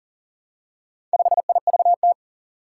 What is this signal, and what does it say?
Morse code, heard as a keyed tone, about 40 wpm, 700 Hz. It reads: 5IVT
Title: Four Challenging Letter-Number Combinations 40wpm